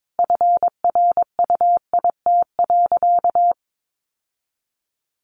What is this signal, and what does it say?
Morse code, heard as a keyed tone, about 22 wpm, 700 Hz. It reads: FRUIT.
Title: Sentences from Top 1300 Words 22wpm